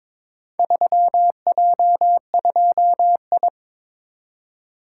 Morse code, heard as a keyed tone, about 23 wpm, 700 Hz. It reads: 3J2I
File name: practice / 22wpm / Four Letter-Number Combinations - 22wpm